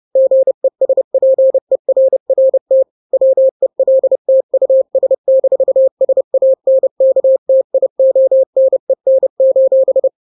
Morse code, heard as a keyed tone, about 30 wpm, 530 Hz. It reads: GESPERRT WELT US-SANKTIONEN:
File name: daily / 29wpm